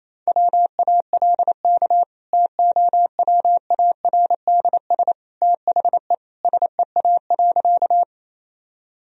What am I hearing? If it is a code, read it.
Morse code, heard as a keyed tone, about 28 wpm, 700 Hz. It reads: WALK TOWARDS THE SEA.
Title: Sentences from Top 1000 Words - Review Part 2 - 28wpm